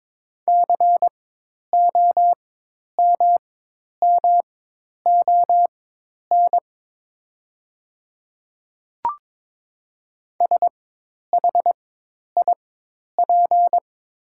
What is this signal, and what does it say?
Morse code, heard as a keyed tone, about 23 wpm, spaced as for 10 wpm, 690 Hz. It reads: COMMON SHIP
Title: 4x Character Spacing - Top 500 Words - 22wpm